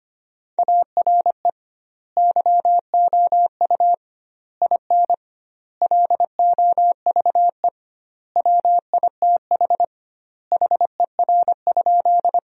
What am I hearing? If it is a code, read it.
Morse code, heard as a keyed tone, about 25 wpm, 700 Hz. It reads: ARE YOU IN LOVE WITH HER?
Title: 2x Word-Spacing - Sentences from Top 400 Words - 25wpm